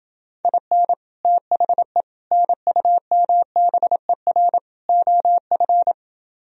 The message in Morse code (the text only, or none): IN THE NUMBER OF